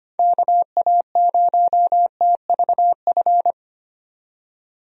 Morse code, heard as a keyed tone, about 25 wpm, 700 Hz. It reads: KA0TVF
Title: USA Call Signs - 6 Characters - 25wpm